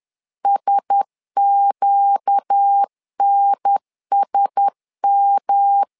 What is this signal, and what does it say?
Morse code, heard as a keyed tone, about 10 wpm, 800 Hz. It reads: SQNSM
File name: LEZIONI / 4 Q1S 10wpm 2133